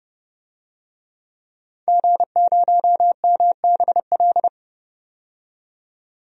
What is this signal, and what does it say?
Morse code, heard as a keyed tone, about 30 wpm, 700 Hz. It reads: G0MBL
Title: DXCC-100 Call Signs - 5 Characters - 30wpm